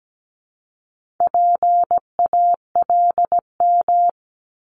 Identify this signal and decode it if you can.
Morse code, heard as a keyed tone, about 17 wpm, 700 Hz. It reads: PALM
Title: Multi-letter Phonemes 17wpm